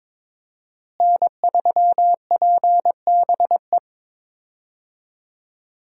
Morse code, heard as a keyed tone, about 21 wpm, 700 Hz. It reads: N3PBE
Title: USA Call Signs - 5 Characters - Repeated 3x - 22wpm